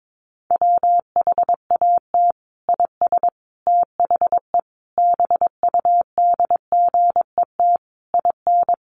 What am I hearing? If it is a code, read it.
Morse code, heard as a keyed tone, about 22 wpm, 700 Hz. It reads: WHAT IS THE BUDGET IN
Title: Sentences from Top 1200 Words 22wpm